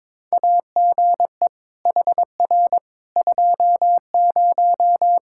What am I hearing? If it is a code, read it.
Morse code, heard as a keyed tone, about 22 wpm, 700 Hz. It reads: AGE HR 20
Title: K1USN SST QSO Practice 22wpm